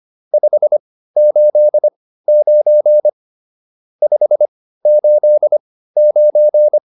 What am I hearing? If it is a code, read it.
Morse code, heard as a keyed tone, about 25 wpm, spaced as for 15 wpm, 600 Hz. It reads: 589 589
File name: QSO403 15